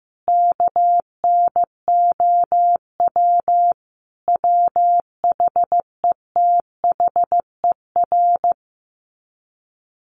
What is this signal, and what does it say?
Morse code, heard as a keyed tone, about 15 wpm, 700 Hz. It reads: KNOW WHETHER
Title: Sets of 5 Words 15wpm